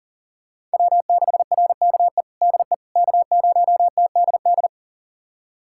Morse code, heard as a keyed tone, about 40 wpm, 700 Hz. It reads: W6RKE DE K0TDD
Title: USA Callsign DE Callsign - 5 Characters - 40wpm